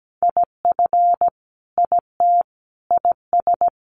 Morse code, heard as a keyed tone, about 17 wpm, 700 Hz. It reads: IF IT IS
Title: Sentences from Top 1100 Words 17wpm